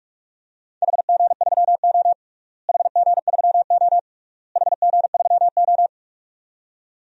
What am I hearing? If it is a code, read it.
Morse code, heard as a keyed tone, about 46 wpm, 700 Hz. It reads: HG3O HG3O HG3O
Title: DXCC-100 Call Signs - 4 Characters - Repeated 3x - 45wpm